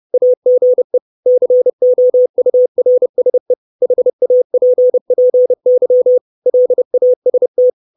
Morse code, heard as a keyed tone, about 30 wpm, 500 Hz. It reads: AGE COURSE HAPPY LAST
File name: 632 Words 30WPM